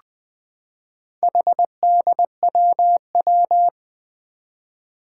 Morse code, heard as a keyed tone, about 20 wpm, 700 Hz. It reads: HDWW
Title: Four Challenging Letter-Number Combinations 20wpm